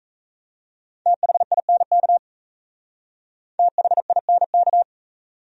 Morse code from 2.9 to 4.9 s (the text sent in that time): THINK